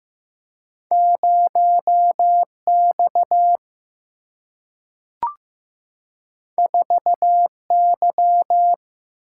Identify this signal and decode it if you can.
Morse code, heard as a keyed tone, about 15 wpm, 700 Hz. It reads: 0X 4Y
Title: Single-Letter Single-Number 15wpm